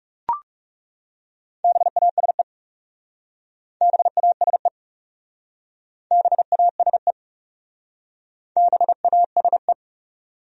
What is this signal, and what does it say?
Morse code, heard as a keyed tone, about 39 wpm, 700 Hz. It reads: BASE BASE BASE BASE